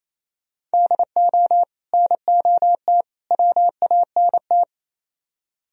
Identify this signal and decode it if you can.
Morse code, heard as a keyed tone, about 35 wpm, 700 Hz. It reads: DO NOT WANT
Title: Speed-Racing - Sets of 3 Words 28wpm